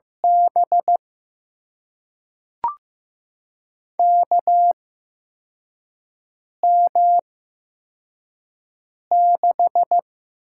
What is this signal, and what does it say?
Morse code, heard as a keyed tone, about 15 wpm, spaced as for 4 wpm, 700 Hz. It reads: B KM6
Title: USA Call Signs - 8x Character Spacing - 15wpm